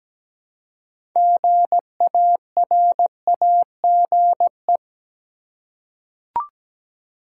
Morse code, heard as a keyed tone, about 17 wpm, 700 Hz. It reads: GARAGE
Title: Six Letter Words - 17wpm